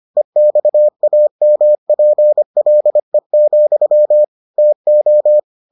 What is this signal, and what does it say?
Morse code, heard as a keyed tone, about 25 wpm, 600 Hz. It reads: EXAMPLE, TO